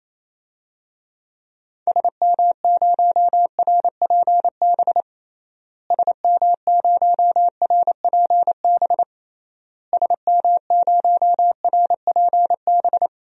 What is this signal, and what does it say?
Morse code, heard as a keyed tone, about 28 wpm, 700 Hz. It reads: SM0RPB SM0RPB SM0RPB
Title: DXCC-100 Call Signs - 6 Characters - Repeated 3x - 28wpm